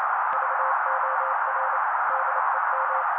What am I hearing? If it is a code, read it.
Morse code, heard as a keyed tone, about 29 wpm, 530 Hz. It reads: VOR DEM
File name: daily / 27wpm noise